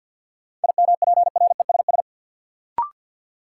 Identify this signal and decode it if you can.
Morse code, heard as a keyed tone, about 53 wpm, 700 Hz. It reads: IMPRESS
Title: Sentences from Top 1600 Words - New Word Drill Alt - 50wpm